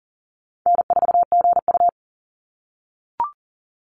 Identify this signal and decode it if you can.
Morse code, heard as a keyed tone, about 40 wpm, 700 Hz. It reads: N4GU
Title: Easy Call Signs - 40wpm